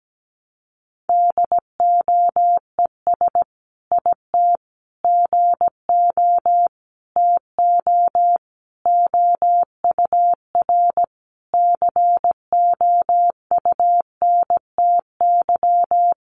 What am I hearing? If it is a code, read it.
Morse code, heard as a keyed tone, about 17 wpm, 700 Hz. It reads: DOES IT GO TO OUR COUNTY
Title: Sentences from Top 1500 Words 17wpm